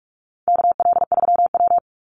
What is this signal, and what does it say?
Morse code, heard as a keyed tone, about 44 wpm, 700 Hz. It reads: KL3W
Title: Call Signs Repeated 2x - 45wpm